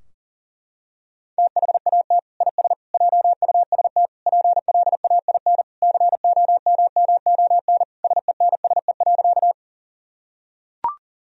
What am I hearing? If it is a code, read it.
Morse code, heard as a keyed tone, about 40 wpm, 700 Hz. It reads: THAT IS JUST PLAIN COMMON SENSE.
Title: Sentences from Top 1000 Words - Review Part 4 - 40wpm